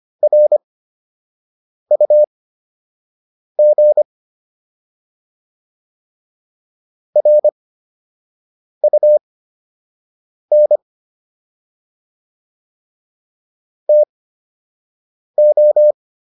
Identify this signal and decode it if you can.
Morse code, heard as a keyed tone, about 25 wpm, spaced as for 6 wpm, 600 Hz. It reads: RUG RUN TO